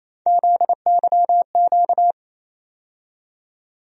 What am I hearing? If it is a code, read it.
Morse code, heard as a keyed tone, about 28 wpm, 700 Hz. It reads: ZYQ
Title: USA Call Sign Postfixes - 3 Letters 28wpm